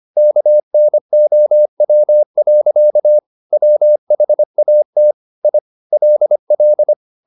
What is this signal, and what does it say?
Morse code, heard as a keyed tone, about 25 wpm, 600 Hz. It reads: KNOW. WHAT I LL
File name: the old man and the sea part 1 25wpm